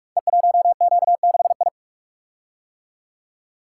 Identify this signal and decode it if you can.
Morse code, heard as a keyed tone, about 44 wpm, 700 Hz. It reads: E1Q6I
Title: Code Groups - 5 Letter:Numbers - 45wpm